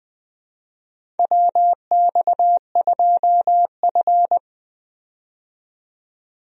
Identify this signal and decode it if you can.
Morse code, heard as a keyed tone, about 20 wpm, 700 Hz. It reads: WX2F
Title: USA Call Signs - 4 Characters - 20wpm